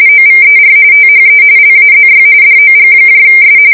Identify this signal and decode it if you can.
RTTY, about 45.45 baud. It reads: OVER THE LAZY DOGS